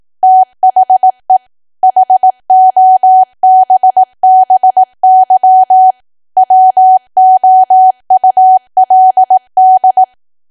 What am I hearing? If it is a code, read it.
Morse code, heard as a keyed tone, about 18 wpm, 750 Hz. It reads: THE HOBBY WOULD
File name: Archive / 18 WPM / 18WPM